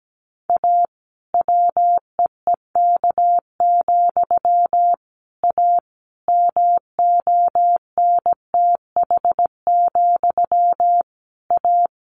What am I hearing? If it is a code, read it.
Morse code, heard as a keyed tone, about 17 wpm, 700 Hz. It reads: A WEEK, A MONTH, A
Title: A Sound Of Thunder 17wpm